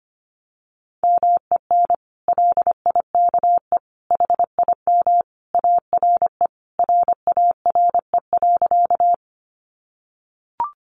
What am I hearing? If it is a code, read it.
Morse code, heard as a keyed tone, about 25 wpm, 700 Hz. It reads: MEN LIKE HIM ARE RARE.
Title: Sentences from Top 1100 Words 25wpm